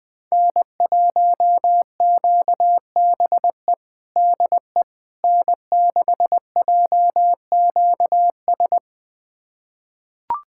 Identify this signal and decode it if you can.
Morse code, heard as a keyed tone, about 21 wpm, 700 Hz. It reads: N1QBE DE N6JQS
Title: USA Callsign DE Callsign - 5 Characters - 20wpm